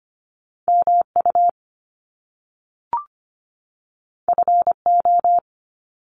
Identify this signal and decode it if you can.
Morse code, heard as a keyed tone, about 25 wpm, 700 Hz. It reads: MU FO
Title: Top Two Letter Combinations in Words 25wpm